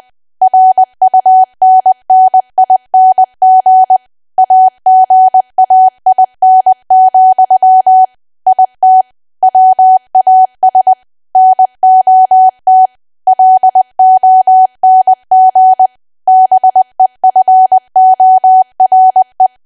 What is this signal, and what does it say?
Morse code, heard as a keyed tone, about 20 wpm, 750 Hz. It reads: RUNNING AGAIN, IT WAS NOT LONG BEFORE
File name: cwa / W1AW-Level3 / 20WPM